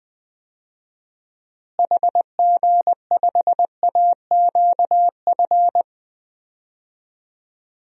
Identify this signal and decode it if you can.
Morse code, heard as a keyed tone, about 21 wpm, 700 Hz. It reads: HG5AQF
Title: DXCC-100 Call Signs - 6 Characters - Repeated 3x - 20wpm